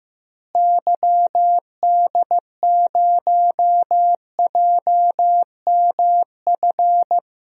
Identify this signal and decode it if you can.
Morse code, heard as a keyed tone, about 15 wpm, 700 Hz. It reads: YD0JMF
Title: DXCC-100 Call Signs - 6 Characters - Repeated 3x - 15wpm